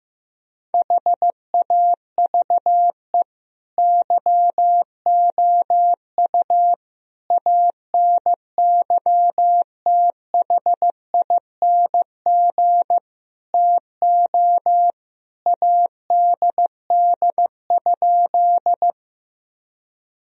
Morse code, heard as a keyed tone, about 15 wpm, 700 Hz. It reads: HAVE YOU ANYTHING TO ADD?
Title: Sentences from Top 1000 Words - Review Part 1 - 15wpm